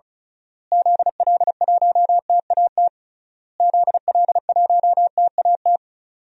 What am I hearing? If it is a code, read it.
Morse code, heard as a keyed tone, about 36 wpm, 700 Hz. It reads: ZL1TAT ZL1TAT
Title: DXCC-100 Call Signs - 6 Characters - Repeated 3x - 35wpm